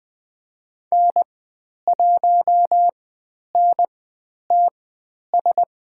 Morse code, heard as a keyed tone, about 20 wpm, spaced as for 10 wpm, 700 Hz. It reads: N1NTS
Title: 109 Lesson 18g - Letter m - Call Signs 20wpm